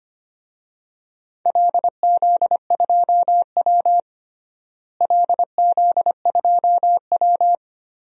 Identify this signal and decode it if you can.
Morse code, heard as a keyed tone, about 25 wpm, 700 Hz. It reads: LZ2W LZ2W
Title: DXCC-100 Call Signs - 4 Characters - Repeated 2x - 25wpm